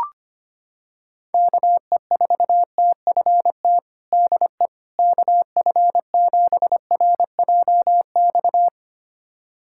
Morse code, heard as a keyed tone, about 25 wpm, 700 Hz. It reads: KE4TFT DE KF7RJX
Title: USA Callsign DE Callsign - 6 Characters - 25wpm